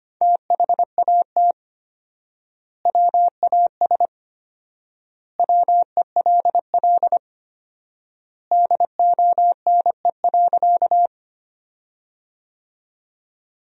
Morse code, heard as a keyed tone, about 25 wpm, 700 Hz. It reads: THAT WAS WELL DONE.
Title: 4x Word-Spacing - Sentences from Top 500 Words - 25wpm